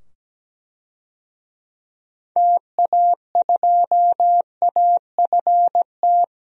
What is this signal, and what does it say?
Morse code, heard as a keyed tone, about 17 wpm, 700 Hz. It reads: TA2AFT
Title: DXCC-100 Call Signs - 6 Characters - 17wpm